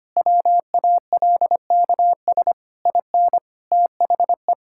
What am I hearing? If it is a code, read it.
Morse code, heard as a keyed tone, about 25 wpm, 700 Hz. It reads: WALKS IN THE
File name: practice / 25wpm / Sentences with 2-Word Phrases - 25wpm